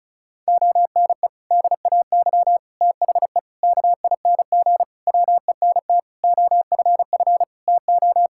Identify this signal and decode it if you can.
Morse code, heard as a keyed tone, about 36 wpm, 700 Hz. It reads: ONE DAY THE KING WENT OFF TO
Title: Sentences from Top 1600 Words 35wpm